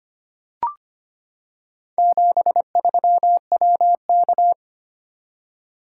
Morse code, heard as a keyed tone, about 25 wpm, 700 Hz. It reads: 73WK
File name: practice / 25wpm / Four Letter-Number Combinations - 25wpm